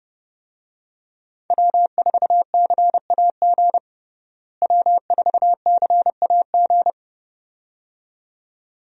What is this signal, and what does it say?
Morse code, heard as a keyed tone, about 30 wpm, 700 Hz. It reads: W4CAG W4CAG
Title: USA Call Signs - 5 Characters - Repeated 2x - 30wpm